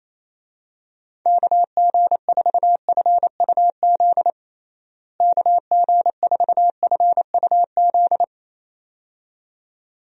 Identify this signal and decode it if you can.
Morse code, heard as a keyed tone, about 28 wpm, 700 Hz. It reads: KG4FUZ KG4FUZ
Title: USA Call Signs - 6 Characters - Repeated 2x - 28wpm